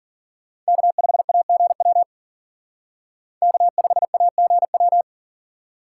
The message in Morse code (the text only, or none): K5AGW K5AGW